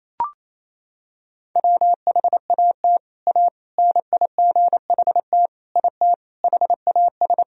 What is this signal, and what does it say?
Morse code, heard as a keyed tone, about 27 wpm, 700 Hz. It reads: WHAT A NIGHT IT HAS
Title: Sentences from Top 300 Words and Abbreviations 28wpm